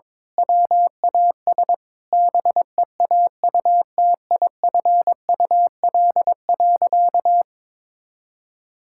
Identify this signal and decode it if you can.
Morse code, heard as a keyed tone, about 21 wpm, 700 Hz. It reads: WAS BEAUTIFUL.
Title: Sentences from Top 1000 Words - Review Part 5 - 22wpm